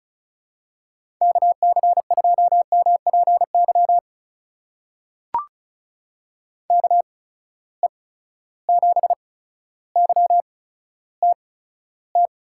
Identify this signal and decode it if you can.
Morse code, heard as a keyed tone, about 36 wpm, spaced as for 15 wpm, 700 Hz. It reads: KC2MPY KE7YTT
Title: USA Call Signs - 6 Characters - 8x Character Spacing - 35wpm